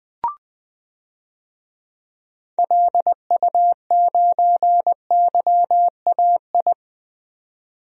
Morse code, heard as a keyed tone, about 21 wpm, 700 Hz. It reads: LU9YAI
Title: DXCC-100 Call Signs - 6 Characters - 20wpm